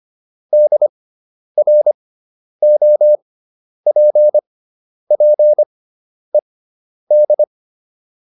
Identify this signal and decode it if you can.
Morse code, heard as a keyed tone, about 25 wpm, spaced as for 10 wpm, 600 Hz. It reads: DROPPED